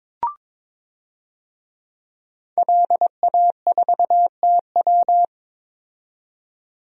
Morse code, heard as a keyed tone, about 22 wpm, 700 Hz. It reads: LA4TW
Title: DXCC-100 Call Signs - 5 Characters - 22wpm